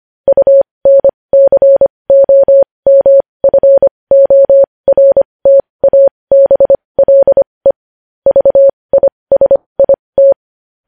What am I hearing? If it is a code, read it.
Morse code, heard as a keyed tone, about 25 wpm, spaced as for 20 wpm, 550 Hz. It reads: UNCOMFORTABLE VISIT
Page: news-headlines-in-morse-code-at-20-wpm